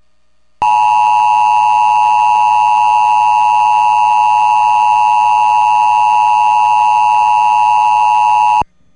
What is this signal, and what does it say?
Emergency Alert System (SAME) tones (the 853 and 960 Hz attention tone). It is heard from 0.6 to 8.7 s.